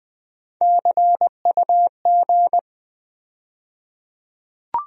Morse code, heard as a keyed tone, about 20 wpm, 700 Hz. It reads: CUG
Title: USA Call Sign Postfixes - 3 Letters 20wpm